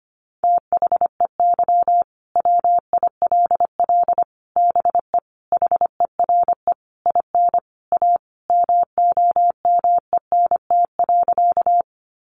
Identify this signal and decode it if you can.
Morse code, heard as a keyed tone, about 25 wpm, 700 Hz. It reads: THEY WILL BE HERE IN A MOMENT.
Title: Sentences from Top 1000 Words 25wpm